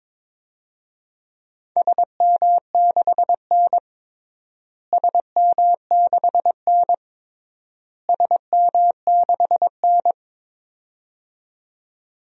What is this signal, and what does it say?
Morse code, heard as a keyed tone, about 22 wpm, 700 Hz. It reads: SM6N SM6N SM6N
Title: DXCC-100 Call Signs - 4 Characters - Repeated 3x - 22wpm